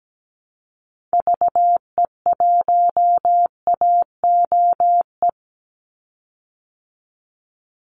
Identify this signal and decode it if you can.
Morse code, heard as a keyed tone, about 17 wpm, 700 Hz. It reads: VE1AOE
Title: Call Signs Repeated 3x - 17wpm